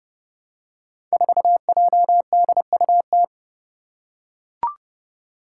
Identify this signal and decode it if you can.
Morse code, heard as a keyed tone, about 30 wpm, 700 Hz. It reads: 4JDUT
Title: Five Challenging Letter-Number Combinations 30wpm